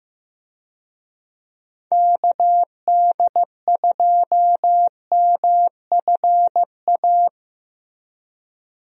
Morse code, heard as a keyed tone, about 14 wpm, 700 Hz. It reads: KD2MFA
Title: USA Call Signs - 6 Characters - Repeated 3x - 15wpm